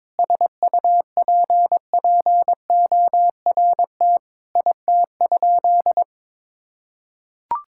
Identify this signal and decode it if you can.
Morse code, heard as a keyed tone, about 22 wpm, 700 Hz. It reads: SUPPORT IT?
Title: Sentences from Top 1000 Words - Review Part 5 - 22wpm